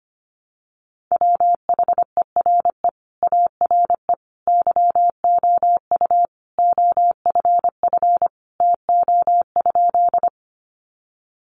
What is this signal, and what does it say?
Morse code, heard as a keyed tone, about 25 wpm, 700 Hz. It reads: WHERE ARE YOU OFF TO?
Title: Sentences from Top 200 Words 25wpm